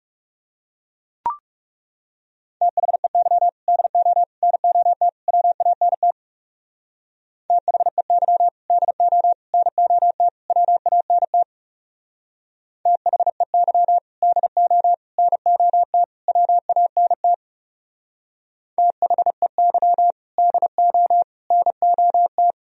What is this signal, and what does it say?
Morse code, heard as a keyed tone, about 37 wpm, 690 Hz. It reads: THEY DO NOT WANT THEY DO NOT WANT THEY DO NOT WANT THEY DO NOT
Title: Speed-Racing - Sets of 4 Words 30wpm